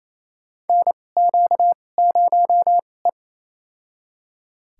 Morse code, heard as a keyed tone, about 28 wpm, spaced as for 20 wpm, 700 Hz. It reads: NQ0E
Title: USA Call Signs - 4 Characters - 2x Character Spacing - 28wpm